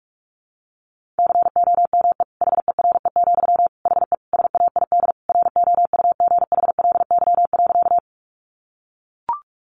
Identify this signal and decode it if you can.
Morse code, heard as a keyed tone, about 44 wpm, 700 Hz. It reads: COME HERE, HE SAID ROUGHLY.
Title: Sentences from Top 1500 Words 45wpm